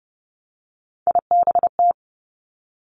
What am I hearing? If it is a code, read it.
Morse code, heard as a keyed tone, about 31 wpm, 700 Hz. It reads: IBT
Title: Three Challenging Letter-Number Combinations 30wpm